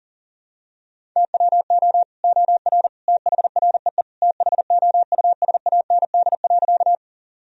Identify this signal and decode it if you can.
Morse code, heard as a keyed tone, about 40 wpm, 700 Hz. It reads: TWO OR THREE THOUSAND.
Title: Sentences from Top 1000 Words - Review Part 3 - 40wpm